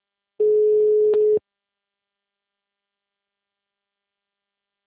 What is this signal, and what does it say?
Ringback tone (Europe, 425 Hz).